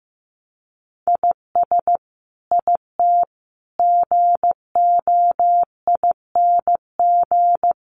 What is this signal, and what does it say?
Morse code, heard as a keyed tone, about 15 wpm, 700 Hz. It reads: IS IT GOING
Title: Sets of 5 Words Encore2 15wpm